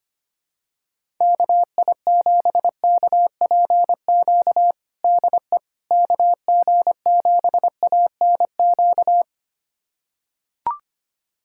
Morse code, heard as a keyed tone, about 25 wpm, 700 Hz. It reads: KI7KPQ DE KG7ANQ
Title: USA Callsign DE Callsign - 6 Characters - 25wpm